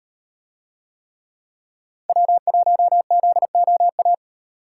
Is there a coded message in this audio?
Yes: Morse code from 2.1 to 4.1 s, heard as a keyed tone, about 38 wpm, 700 Hz. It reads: W1ZOA